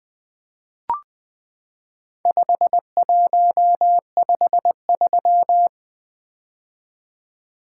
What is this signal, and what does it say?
Morse code, heard as a keyed tone, about 20 wpm, 700 Hz. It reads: 5153
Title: Four Letter-Number Combinations - 20wpm